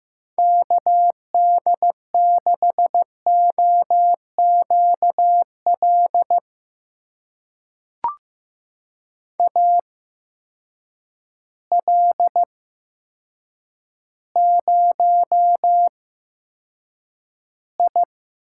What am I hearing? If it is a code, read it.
Morse code, heard as a keyed tone, about 15 wpm, spaced as for 8 wpm, 700 Hz. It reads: KD6OQL AL0I
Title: USA Call Signs - 8x Character Spacing - 15wpm